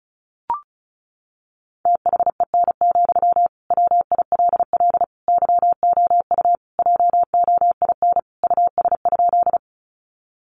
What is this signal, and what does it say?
Morse code, heard as a keyed tone, about 34 wpm, 700 Hz. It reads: THEN, WILL YOU JOIN US?
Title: Sentences from Top 1400 Words 35wpm